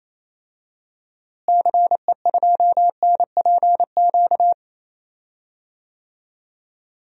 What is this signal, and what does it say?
Morse code, heard as a keyed tone, about 27 wpm, 700 Hz. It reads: CE2NPQ
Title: DXCC-100 Call Signs - 6 Characters - Repeated 3x - 28wpm